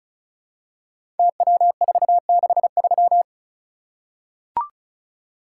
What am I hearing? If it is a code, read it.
Morse code, heard as a keyed tone, about 36 wpm, 700 Hz. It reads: TW463
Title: Code Groups - 5 Letter:Numbers - 35wpm